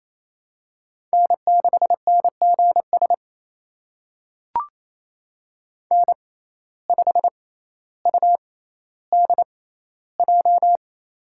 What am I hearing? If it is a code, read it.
Morse code, heard as a keyed tone, about 28 wpm, spaced as for 14 wpm, 700 Hz. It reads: N6NGS N5UDJ
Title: USA Call Signs - 6x Character Spacing - 28wpm